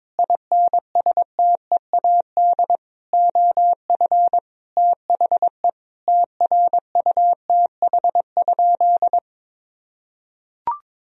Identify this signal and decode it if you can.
Morse code, heard as a keyed tone, about 22 wpm, 700 Hz. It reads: INSTEAD OF THE TRUTH?
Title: Sentences from Top 1000 Words - Review Part 4 - 22wpm